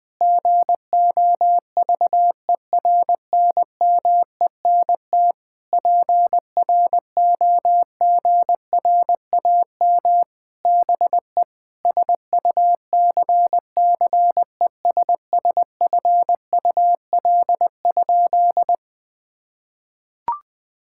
Morse code, heard as a keyed tone, about 20 wpm, 700 Hz. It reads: GOVERNMENT PROGRAM BE SUCCESSFUL?
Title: Sentences from Top 1000 Words - Review Part 5 - 20wpm